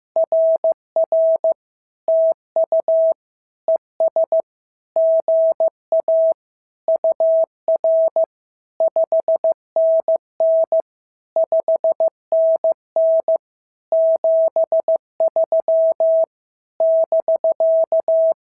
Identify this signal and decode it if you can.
Morse code, heard as a keyed tone, about 15 wpm, 690 Hz. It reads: RR TU ES GA UR 5NN 5NN 73 <BK>
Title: SOTA QSO Practice 15wpm